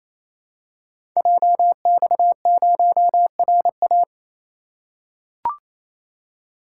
Morse code, heard as a keyed tone, about 28 wpm, 700 Hz. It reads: JX0RA
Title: Code Groups - 5 Letter:Numbers - 28wpm